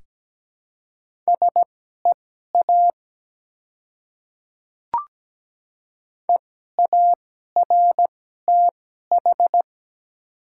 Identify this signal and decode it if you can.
Morse code, heard as a keyed tone, about 17 wpm, spaced as for 12 wpm, 700 Hz. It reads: SEA EARTH
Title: 2x Character Spacing - Top 300 Words - 17wpm